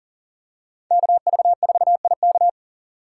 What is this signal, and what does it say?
Morse code, heard as a keyed tone, about 40 wpm, 700 Hz. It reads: KV4IK
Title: USA Call Signs - 5 Characters - 2x Character Spacing - 40wpm